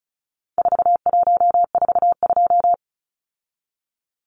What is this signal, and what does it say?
Morse code, heard as a keyed tone, about 34 wpm, 700 Hz. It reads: 4142
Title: Numbers - 4-Digit 35wpm